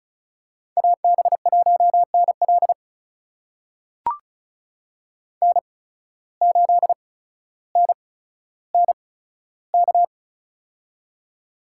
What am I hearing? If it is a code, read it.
Morse code, heard as a keyed tone, about 34 wpm, spaced as for 15 wpm, 700 Hz. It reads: AB1NL N8NNK
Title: USA Call Signs - 5 Characters - 8x Character Spacing - 35wpm